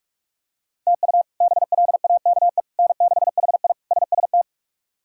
Morse code, heard as a keyed tone, about 41 wpm, 700 Hz. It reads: TU BLAKE N6HI SST
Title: K1USN SST QSO Practice - 45wpm